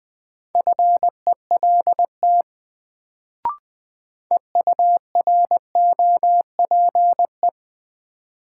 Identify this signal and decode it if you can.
Morse code, heard as a keyed tone, about 21 wpm, 700 Hz. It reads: FELT EUROPE
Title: Sets of 5 Words Encore - New Word Drill - 20wpm